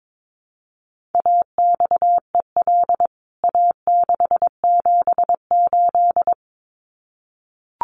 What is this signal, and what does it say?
Morse code, heard as a keyed tone, about 22 wpm, 700 Hz. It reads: AXEL A678
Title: CWT Exchange 22wpm